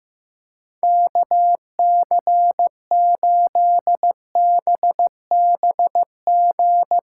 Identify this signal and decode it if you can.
Morse code, heard as a keyed tone, about 16 wpm, 700 Hz. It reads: KC8BBG
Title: USA Callsign DE Callsign 15wpm